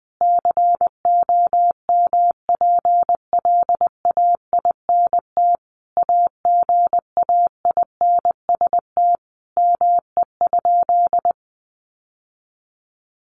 Morse code, heard as a keyed tone, about 20 wpm, 700 Hz. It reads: COMPLAINT AGAINST ME?
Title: Sentences from Top 1400 Words 20wpm